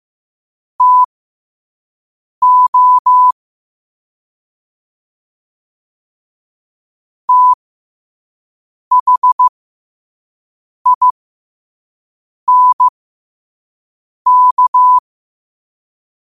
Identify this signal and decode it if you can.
Morse code, heard as a keyed tone, about 15 wpm, spaced as for 5 wpm, 1000 Hz. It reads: TO THINK
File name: qotdfiles / 5wpm